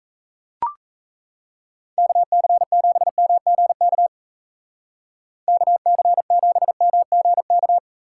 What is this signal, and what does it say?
Morse code, heard as a keyed tone, about 37 wpm, 700 Hz. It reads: KC7MGK KC7MGK